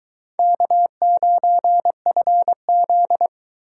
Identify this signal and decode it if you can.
Morse code, heard as a keyed tone, about 24 wpm, 700 Hz. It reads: K9FZ